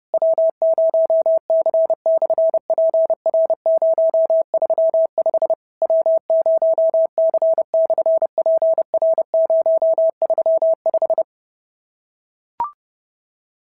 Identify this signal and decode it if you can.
Morse code, heard as a keyed tone, about 30 wpm, 650 Hz. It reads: W0C/PR035 W0C/PR035
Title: SOTA QSO Practice 30wpm